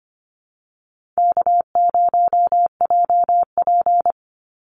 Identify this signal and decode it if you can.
Morse code, heard as a keyed tone, about 25 wpm, 700 Hz. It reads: K0JP
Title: Call Signs Repeated 2x - 25wpm